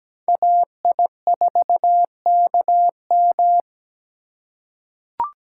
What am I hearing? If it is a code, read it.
Morse code, heard as a keyed tone, about 17 wpm, 700 Hz. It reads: AI4KM
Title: USA Call Signs - 5 Characters - 2x Character Spacing - 17wpm